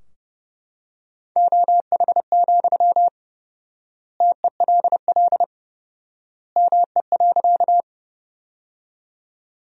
Morse code, heard as a keyed tone, about 30 wpm, 700 Hz. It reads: OH, TELL ME.
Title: 4x Word-Spacing - Sentences from Top 500 Words - 30wpm